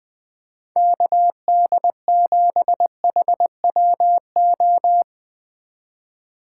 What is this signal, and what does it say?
Morse code, heard as a keyed tone, about 21 wpm, 700 Hz. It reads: KD7HWO
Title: Speed Racing - USA 2x3 Call Signs 20wpm